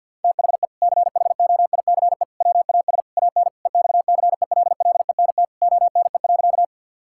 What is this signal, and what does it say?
Morse code, heard as a keyed tone, about 50 wpm, 700 Hz. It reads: THE CHOICE WAS AN EXCELLENT ONE.
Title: Sentences from Top 1000 Words - Review Part 6 - 50wpm